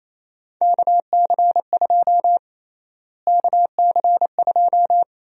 Morse code, heard as a keyed tone, about 27 wpm, 700 Hz. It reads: KC2 KC2
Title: USA Call Sign Prefixes - Repeated 2x - 28wpm